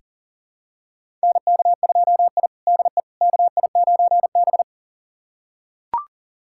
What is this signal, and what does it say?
Morse code, heard as a keyed tone, about 40 wpm, 700 Hz. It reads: NK2I DE KI9B
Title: USA Callsign DE Callsign - 4 Characters - 40wpm